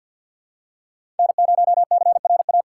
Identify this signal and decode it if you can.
Morse code, heard as a keyed tone, about 51 wpm, 700 Hz. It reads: N0CRA
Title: USA Call Signs - 3x Character Spacing - 50wpm